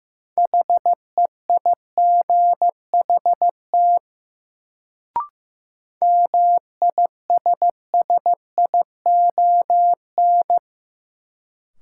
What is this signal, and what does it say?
Morse code, heard as a keyed tone, about 15 wpm, 700 Hz. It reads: HEIGHT MISSION
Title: Sentences from Top 1600 Words - New Word Drill - 15wpm